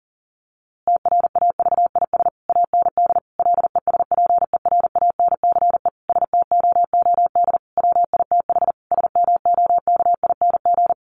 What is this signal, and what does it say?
Morse code, heard as a keyed tone, about 40 wpm, 700 Hz. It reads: TRAVIS AND LESPERANCE STOOD WITH SMOKING
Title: A Sound Of Thunder 40wpm